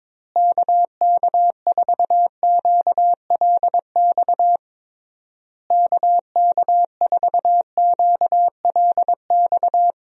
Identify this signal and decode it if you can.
Morse code, heard as a keyed tone, about 22 wpm, 700 Hz. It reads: KK4QLX KK4QLX
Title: USA Call Signs - 6 Characters - Repeated 3x - 22wpm